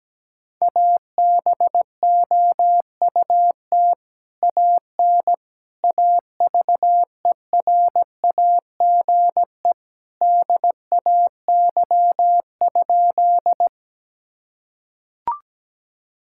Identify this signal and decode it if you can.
Morse code, heard as a keyed tone, about 17 wpm, 700 Hz. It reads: ABOUT AN AVERAGE DAY?
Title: Sentences from Top 1000 Words - Review Part 6 - 17wpm